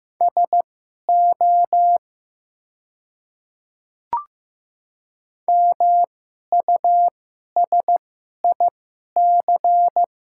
Morse code, heard as a keyed tone, about 15 wpm, spaced as for 11 wpm, 700 Hz. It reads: SO MUSIC
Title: 2x Character Spacing - Top 300 Words - 15wpm